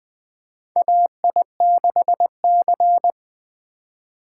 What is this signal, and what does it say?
Morse code, heard as a keyed tone, about 20 wpm, 700 Hz. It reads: AI6C
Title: USA Call Signs - 4 Characters - 4x Character Spacing - 20wpm